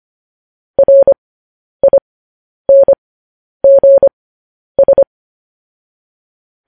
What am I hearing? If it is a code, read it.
Morse code, heard as a keyed tone, about 25 wpm, spaced as for 10 wpm, 550 Hz. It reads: RINGS
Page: news headlines in morse code at 10 wpm